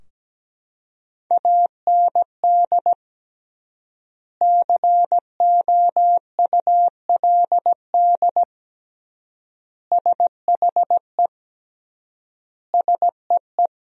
Morse code, heard as a keyed tone, about 17 wpm, 700 Hz. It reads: AND COULD SHE SEE
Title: 3x Word-Spacing - Sentences from Top 100 Words - 17wpm